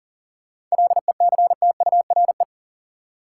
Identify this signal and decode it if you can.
Morse code, heard as a keyed tone, about 40 wpm, 700 Hz. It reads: LECTURE
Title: Seven Letter Words - 40wpm